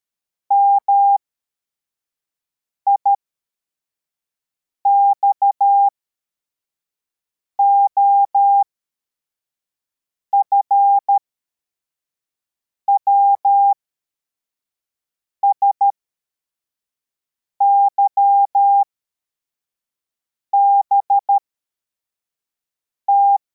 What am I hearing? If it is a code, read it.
Morse code, heard as a keyed tone, about 13 wpm, spaced as for 4 wpm, 800 Hz. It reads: MIXOF WSYBT
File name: formation / cw / Text64